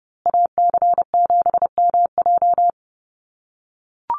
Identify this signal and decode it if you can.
Morse code, heard as a keyed tone, about 30 wpm, 700 Hz. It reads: AC7MJ
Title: USA Call Signs Repeated 2x 30wpm